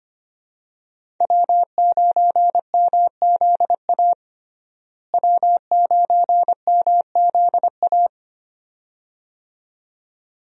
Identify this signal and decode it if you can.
Morse code, heard as a keyed tone, about 25 wpm, 700 Hz. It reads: W9MZA W9MZA
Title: USA Call Signs - 5 Characters - Repeated 2x - 25wpm